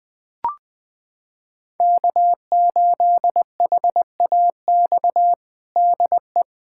Morse code, heard as a keyed tone, about 21 wpm, 700 Hz. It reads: K8HAX DE
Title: USA Callsign DE Callsign - 5 Characters - 20wpm